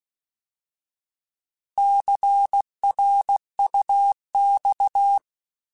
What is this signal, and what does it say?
Morse code, heard as a keyed tone, about 16 wpm, 800 Hz. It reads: CRUX
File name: Alphabetical / Alphabetical mp3s / Cs @15